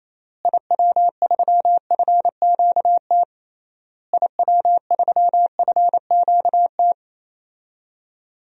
Morse code, heard as a keyed tone, about 27 wpm, 700 Hz. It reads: IW3FQT IW3FQT
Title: DXCC-100 Call Signs - 6 Characters - Repeated 3x - 28wpm